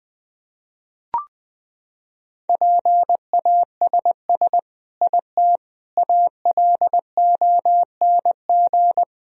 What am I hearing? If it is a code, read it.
Morse code, heard as a keyed tone, about 20 wpm, 700 Hz. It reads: PASS IT ALONG
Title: Sentences from Top 1000 Words - Review Part 1 - 20wpm